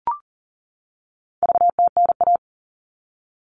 Morse code, heard as a keyed tone, about 40 wpm, 700 Hz. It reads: VTNA
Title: All Cut Numbers - 4-Digit 40wpm